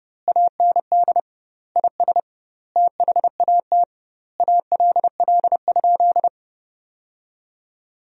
Morse code, heard as a keyed tone, about 32 wpm, 700 Hz. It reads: AND IS THAT ALL?
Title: 2x Word-Spacing - Sentences from Top 200 Words - 30wpm